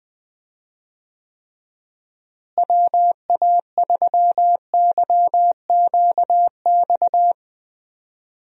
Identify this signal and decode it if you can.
Morse code, heard as a keyed tone, about 21 wpm, 700 Hz. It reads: WA3YQX
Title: USA Call Signs - 6 Characters - Repeated 2x - 20wpm